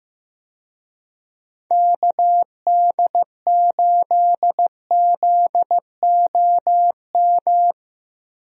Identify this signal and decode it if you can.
Morse code, heard as a keyed tone, about 15 wpm, 700 Hz. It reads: KD8ZOM
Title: USA Call Signs - 6 Characters - 15wpm